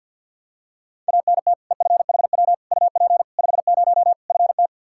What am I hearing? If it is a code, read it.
Morse code, heard as a keyed tone, about 50 wpm, 700 Hz. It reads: ANT EFHW UP 50 FT
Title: Practice QSO - Exchange 2 - Info - 50wpm